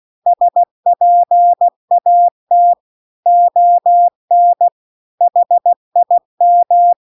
Morse code, heard as a keyed tone, about 16 wpm, 700 Hz. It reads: SPAT ON HIM